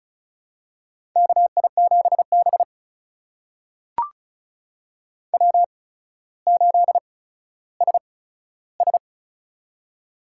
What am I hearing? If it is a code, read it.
Morse code, heard as a keyed tone, about 36 wpm, spaced as for 15 wpm, 700 Hz. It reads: KI7B W8SS
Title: USA Call Signs - 4 Characters - 8x Character Spacing - 35wpm